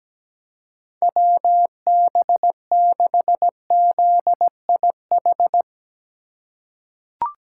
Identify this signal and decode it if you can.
Morse code, heard as a keyed tone, about 17 wpm, 700 Hz. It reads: WB6ZIH
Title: USA Call Signs - 6 Characters - 8x Character Spacing - 17wpm